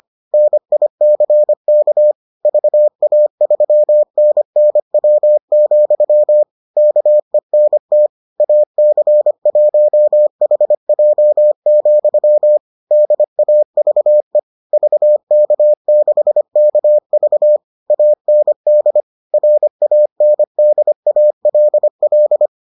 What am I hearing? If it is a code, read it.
Morse code, heard as a keyed tone, about 25 wpm, 600 Hz. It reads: NICK VA3NNW, KENT AC1HJ, DAVE VK6KV AND RANDALL